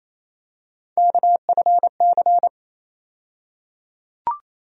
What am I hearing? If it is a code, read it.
Morse code, heard as a keyed tone, about 28 wpm, 700 Hz. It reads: KFC
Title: USA Call Sign Postfixes - 3 Letters 28wpm